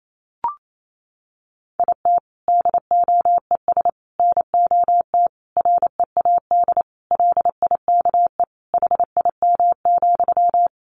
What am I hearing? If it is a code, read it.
Morse code, heard as a keyed tone, about 28 wpm, 700 Hz. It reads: IT DOES NOT READ LIKE HIM,
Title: Sentences from Top 800 Words 28wpm